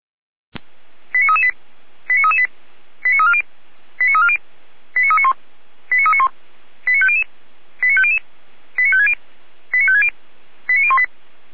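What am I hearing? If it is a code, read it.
Five-tone selective call (ZVEI), heard as quick runs of five tones, tones of about 70 ms. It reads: 89338 89338 89340 89340 89391 89391 89500 89500 89560 89560 89018
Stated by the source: Five-tone ZVEI selective call received by SDR.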